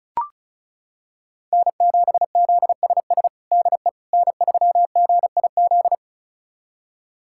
Morse code, heard as a keyed tone, about 36 wpm, 700 Hz. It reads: N7ZSS DE N3GIZ
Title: USA Callsign DE Callsign - 5 Characters - 35wpm